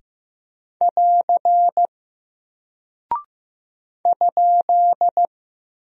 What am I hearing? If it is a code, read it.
Morse code, heard as a keyed tone, about 15 wpm, 700 Hz. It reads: <AR> ?
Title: Single Character (Letter:Number:Punctuation:Prosign) Alt 15wpm